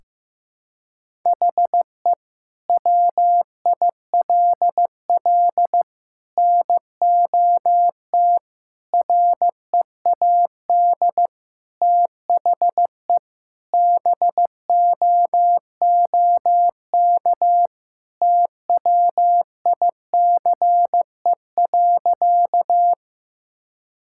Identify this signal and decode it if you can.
Morse code, heard as a keyed tone, about 15 wpm, 700 Hz. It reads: HE WILL NOT READ THE BOOK TWICE.
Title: Sentences from Top 1600 Words 15wpm